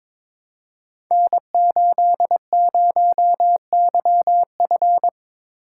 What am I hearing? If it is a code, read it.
Morse code, heard as a keyed tone, about 22 wpm, 700 Hz. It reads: N80YF
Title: Code Groups - 5 Letter:Numbers - 22wpm